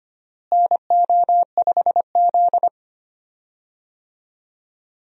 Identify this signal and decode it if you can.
Morse code, heard as a keyed tone, about 25 wpm, 700 Hz. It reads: NO5Z
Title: USA Call Signs - 4 Characters - Repeated 3x - 25wpm